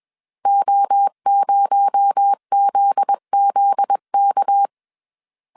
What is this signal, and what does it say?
Morse code, heard as a keyed tone, about 21 wpm, 800 Hz. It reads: O0ZZK